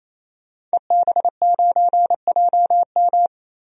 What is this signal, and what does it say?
Morse code, heard as a keyed tone, about 28 wpm, 700 Hz. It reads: EB9JM
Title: DXCC-100 Call Signs - 5 Characters - 28wpm